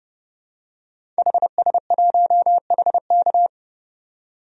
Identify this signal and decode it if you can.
Morse code, heard as a keyed tone, about 30 wpm, 700 Hz. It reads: HS1HK
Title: DXCC-100 Call Signs - 5 Characters - 30wpm